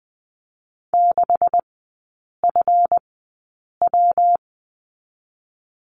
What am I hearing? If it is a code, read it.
Morse code, heard as a keyed tone, about 20 wpm, spaced as for 8 wpm, 700 Hz. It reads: 6FW
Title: 172 Lesson 28e - Number 9 - Three Characters 20wpm